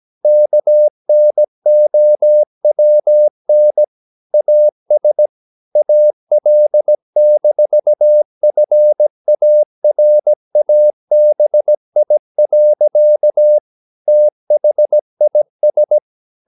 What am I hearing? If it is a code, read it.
Morse code, heard as a keyed tone, about 17 wpm, 600 Hz. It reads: KNOWN AS AL-FARABI. THIS
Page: Romanen Never af Ken Follet på engelsk.